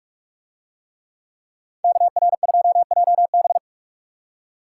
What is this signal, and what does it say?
Morse code, heard as a keyed tone, about 44 wpm, 700 Hz. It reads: KR2JB